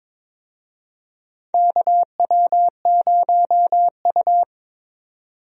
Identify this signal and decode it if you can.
Morse code, heard as a keyed tone, about 22 wpm, 700 Hz. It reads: KW0U
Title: USA Call Signs - 4 Characters - 22wpm